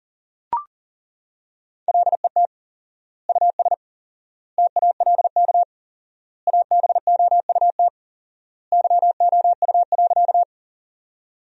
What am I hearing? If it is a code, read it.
Morse code, heard as a keyed tone, about 40 wpm, 700 Hz. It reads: LET US TALK ABOUT YOU.
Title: 4x Word-Spacing - Sentences from Top 400 Words - 40wpm